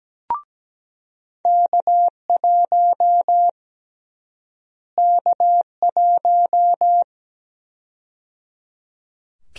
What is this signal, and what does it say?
Morse code, heard as a keyed tone, about 17 wpm, 700 Hz. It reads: K1 K1
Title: USA Call Sign Prefixes - Repeated 2x - 17wpm